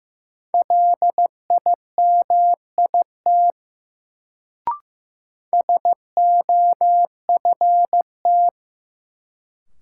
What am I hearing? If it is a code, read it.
Morse code, heard as a keyed tone, about 15 wpm, 700 Hz. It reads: LIMIT SOFT
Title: Sentences from Top 1500 Words - New Word Drill - 15wpm